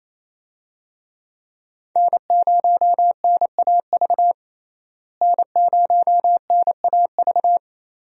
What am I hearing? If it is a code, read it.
Morse code, heard as a keyed tone, about 28 wpm, 700 Hz. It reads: N0NAV N0NAV
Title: USA Call Signs - 5 Characters - Repeated 3x - 28wpm